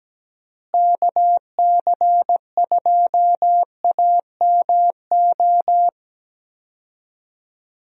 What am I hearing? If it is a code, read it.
Morse code, heard as a keyed tone, about 17 wpm, 700 Hz. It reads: KC2AMO
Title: USA Call Signs - 6 Characters - Repeated 2x - 17wpm